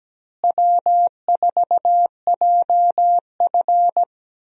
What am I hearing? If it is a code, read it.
Morse code, heard as a keyed tone, about 17 wpm, 700 Hz. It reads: W4JF
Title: USA Call Signs - 4 Characters - Repeated 3x - 17wpm